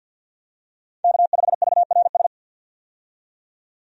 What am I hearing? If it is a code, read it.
Morse code, heard as a keyed tone, about 46 wpm, 700 Hz. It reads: K5VRS
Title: USA Call Signs - 5 Characters - 50wpm